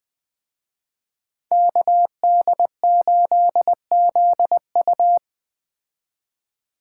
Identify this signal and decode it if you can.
Morse code, heard as a keyed tone, about 20 wpm, 700 Hz. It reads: KD8ZU
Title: Speed Racing - USA 2x2 Call Signs 17wpm